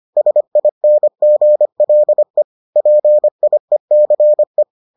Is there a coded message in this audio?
Yes: Morse code from 0.2 to 4.6 s, heard as a keyed tone, about 25 wpm, 600 Hz. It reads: SINGLE PIECE